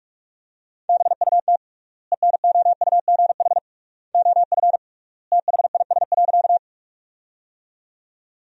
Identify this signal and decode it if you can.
Morse code, heard as a keyed tone, about 46 wpm, 700 Hz. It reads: BUT ENOUGH OF THIS.
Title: 3x Word-Spacing - Sentences from Top 400 Words - 45wpm